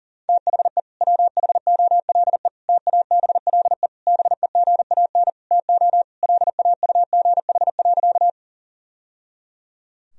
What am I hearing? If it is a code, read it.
Morse code, heard as a keyed tone, about 37 wpm, 700 Hz. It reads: THE WHOLE TABLE BEGAN TO LAUGH.
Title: Sentences from Top 1000 Words - Review Part 2 - 40wpm